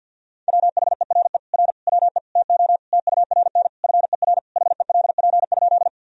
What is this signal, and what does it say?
Morse code, heard as a keyed tone, about 50 wpm, 700 Hz. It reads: WHERE R WE TO TURN FER HELP?
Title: Sentences from Top 200 Words and Abbreviations 50wpm